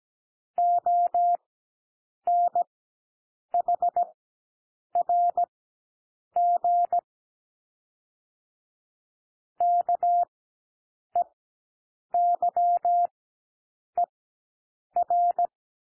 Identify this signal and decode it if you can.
Morse code, heard as a keyed tone, about 17 wpm, spaced as for 7 wpm, 700 Hz. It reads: ONHRG KEYER
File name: SOUND LETTERS 17 wpm